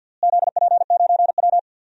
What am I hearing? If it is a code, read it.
Morse code, heard as a keyed tone, about 53 wpm, 700 Hz. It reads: ZP9W
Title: DXCC-100 Call Signs - 4 Characters - Repeated 3x - 50wpm